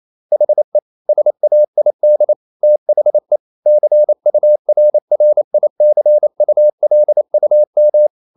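Morse code, heard as a keyed tone, about 28 wpm, 600 Hz. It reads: HE SAID THE CURRICULUM
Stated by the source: Morse code practice at 28 wpm, text of Ken Follett's novel Never in English, from CW med Gnister.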